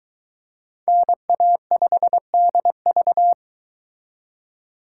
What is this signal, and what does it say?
Morse code, heard as a keyed tone, about 22 wpm, 700 Hz. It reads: NA5DV